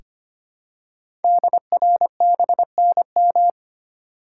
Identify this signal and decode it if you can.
Morse code, heard as a keyed tone, about 25 wpm, 700 Hz. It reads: DRBNM
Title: Code Groups - 5 Letters - 25wpm